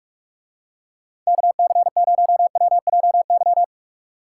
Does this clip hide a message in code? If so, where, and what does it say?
Yes: Morse code from 1.3 to 3.6 s, heard as a keyed tone, about 45 wpm, 700 Hz. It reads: KC0WJY